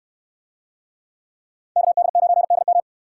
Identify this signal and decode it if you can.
Morse code, heard as a keyed tone, about 63 wpm, 700 Hz. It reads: KD9DM